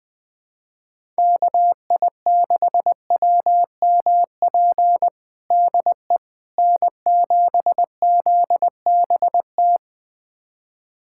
Morse code, heard as a keyed tone, about 21 wpm, 700 Hz. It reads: KI6WMP DE N7ZBT
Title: USA Callsign DE Callsign 20wpm